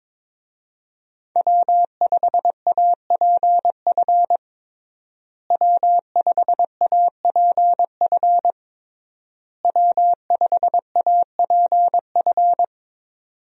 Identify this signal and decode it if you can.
Morse code, heard as a keyed tone, about 22 wpm, 700 Hz. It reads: W5APF W5APF W5APF
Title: USA Call Signs - 5 Characters - Repeated 3x - 22wpm